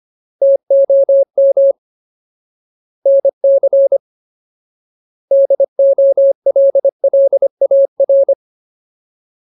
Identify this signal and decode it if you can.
Morse code, heard as a keyed tone, about 25 wpm, 550 Hz. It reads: TOM NC DOLLAR
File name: Ultimate Word List / 25WPM Ultimate